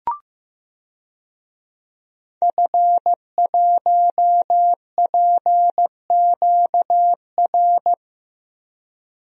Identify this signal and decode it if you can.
Morse code, heard as a keyed tone, about 15 wpm, 700 Hz. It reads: F1PQR
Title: DXCC-100 Call Signs - 5 Characters - 15wpm